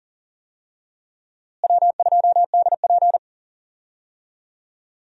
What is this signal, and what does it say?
Morse code, heard as a keyed tone, about 37 wpm, 700 Hz. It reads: W2DP
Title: USA Call Signs - 4 Characters - Repeated 3x - 40wpm